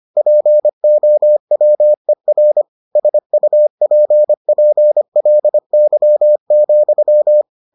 Morse code, heard as a keyed tone, about 25 wpm, 600 Hz. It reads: POWER SUPPLY,